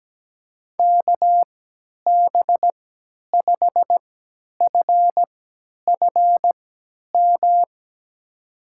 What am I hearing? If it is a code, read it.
Morse code, heard as a keyed tone, about 17 wpm, spaced as for 10 wpm, 700 Hz. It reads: KB5FFM
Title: USA Call Signs - 6 Characters - 3x Character Spacing - 17wpm